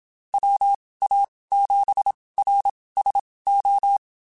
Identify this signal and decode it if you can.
Morse code, heard as a keyed tone, about 27 wpm, spaced as for 19 wpm, 800 Hz. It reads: WA7RSO
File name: QSOs / QSOs-mp3s / WD0FBT 1@25